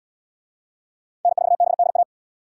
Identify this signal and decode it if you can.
Morse code, heard as a keyed tone, about 68 wpm, 700 Hz. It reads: N4BDA